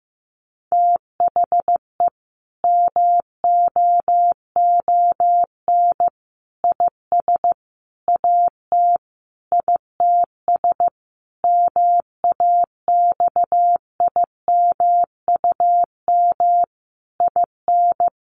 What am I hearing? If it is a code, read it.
Morse code, heard as a keyed tone, about 15 wpm, 700 Hz. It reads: THE MOON IS AT ITS MAXIMUM IN
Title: Sentences from Top 1300 Words 15wpm